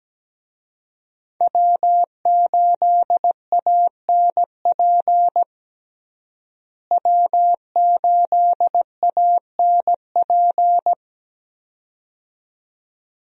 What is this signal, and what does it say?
Morse code, heard as a keyed tone, about 17 wpm, 700 Hz. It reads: W8ANP W8ANP
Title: USA Call Signs - 5 Characters - Repeated 2x - 17wpm